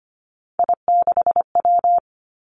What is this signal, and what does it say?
Morse code, heard as a keyed tone, about 25 wpm, 700 Hz. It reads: I6W
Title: Three Challenging Letter-Number Combinations 25wpm